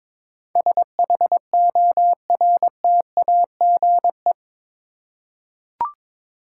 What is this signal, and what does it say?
Morse code, heard as a keyed tone, about 22 wpm, 700 Hz. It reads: SHORTAGE
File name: practice / 22wpm / Eight Letter Words - 22wpm